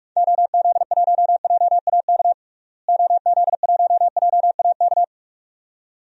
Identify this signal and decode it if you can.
Morse code, heard as a keyed tone, about 46 wpm, 700 Hz. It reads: OZ1JAK OZ1JAK
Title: DXCC-100 Call Signs - 6 Characters - Repeated 2x - 45wpm